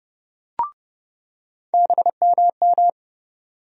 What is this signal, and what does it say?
Morse code, heard as a keyed tone, about 29 wpm, 700 Hz. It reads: BMM
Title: USA Call Sign Postfixes - 3 Letters 30wpm